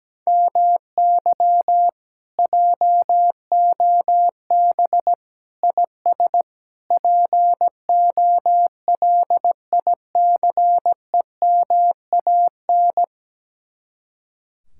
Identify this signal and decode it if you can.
Morse code, heard as a keyed tone, about 17 wpm, 700 Hz. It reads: MY JOB IS POLICEMAN
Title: Speed-Racing Common QSO Phrases Encore - 17wpm